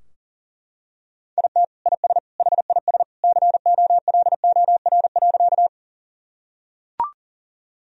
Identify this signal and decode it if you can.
Morse code, heard as a keyed tone, about 40 wpm, 700 Hz. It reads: IT IS HIS COLOR.
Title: Sentences from Top 1000 Words - Review Part 2 - 40wpm